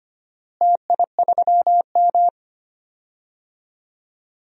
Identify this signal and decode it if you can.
Morse code, heard as a keyed tone, about 25 wpm, 700 Hz. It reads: TI3M
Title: DXCC-100 Call Signs - 4 Characters - Repeated 2x - 25wpm